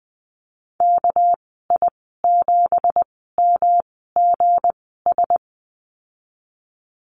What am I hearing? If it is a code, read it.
Morse code, heard as a keyed tone, about 20 wpm, spaced as for 14 wpm, 700 Hz. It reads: KI7MGS
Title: 207 Lesson 33g - Letter j - Call Signs 20wpm